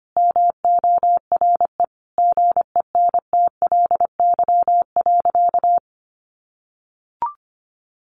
Morse code, heard as a keyed tone, about 25 wpm, 700 Hz. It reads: MORE GENTLY.
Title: Sentences from Top 1500 Words 25wpm